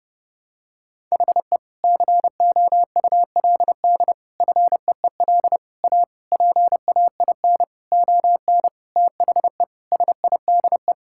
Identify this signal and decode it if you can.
Morse code, heard as a keyed tone, about 30 wpm, 700 Hz. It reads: HE COULD FEEL A PAIN ON THE SIDE
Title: Sentences from Top 1600 Words 30wpm